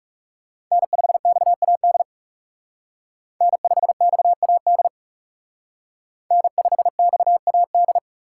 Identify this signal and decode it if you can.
Morse code, heard as a keyed tone, about 34 wpm, 700 Hz. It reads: N5XAD N5XAD N5XAD
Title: Speed Racing - USA 1x3 Call Signs 30wpm